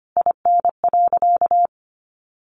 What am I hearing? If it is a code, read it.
Morse code, heard as a keyed tone, about 25 wpm, 700 Hz. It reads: IN.
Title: Sentences from Top 1200 Words 25wpm